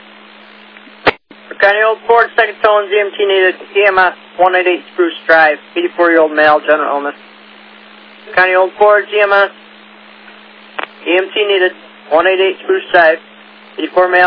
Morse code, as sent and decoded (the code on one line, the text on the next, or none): none